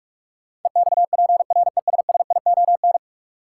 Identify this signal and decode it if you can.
Morse code, heard as a keyed tone, about 46 wpm, 700 Hz. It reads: EXPRESSION
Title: Sentences from Top 1200 Words - New Word Drill - 45wpm